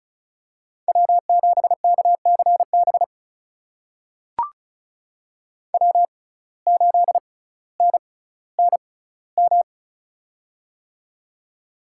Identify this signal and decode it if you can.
Morse code, heard as a keyed tone, about 36 wpm, spaced as for 18 wpm, 700 Hz. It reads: W7KCB W8NNM
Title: USA Call Signs - 6x Character Spacing - 35wpm